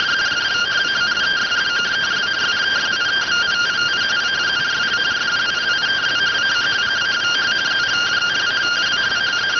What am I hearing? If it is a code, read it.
Two-tone teleprinter signal, about 170 Hz shift.